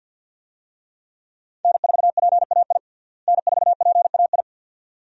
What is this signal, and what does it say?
Morse code, heard as a keyed tone, about 50 wpm, 700 Hz. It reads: N4PAI N4PAI
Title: USA Call Signs - 5 Characters - Repeated 2x - 50wpm